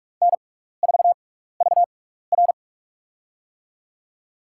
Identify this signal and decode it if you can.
Morse code, heard as a keyed tone, about 44 wpm, spaced as for 15 wpm, 700 Hz. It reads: N4VR
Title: USA Call Signs - 6x Character Spacing - 45wpm